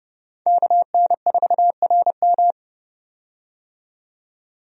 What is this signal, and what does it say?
Morse code, heard as a keyed tone, about 31 wpm, 700 Hz. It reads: KN4RM
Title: USA Call Signs - 5 Characters - 30wpm